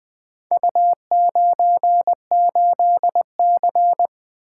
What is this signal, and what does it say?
Morse code, heard as a keyed tone, about 21 wpm, 700 Hz. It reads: U98C
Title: Four Letter-Number Combinations - 20wpm